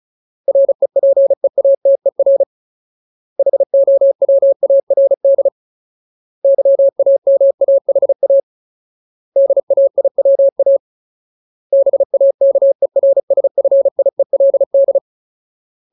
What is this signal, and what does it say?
Morse code, heard as a keyed tone, about 34 wpm, 550 Hz. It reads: REPEATER HOWARD YAMAHA DAIWA BAKERSFIELD
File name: Ultimate Word List / 35WPM Ultimate